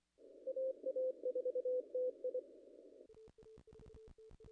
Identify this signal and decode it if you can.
Morse code, heard as a keyed tone, about 24 wpm, 500 Hz. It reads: AA4TI
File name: contests / ss-cw-k5tr-op-n5rz -0000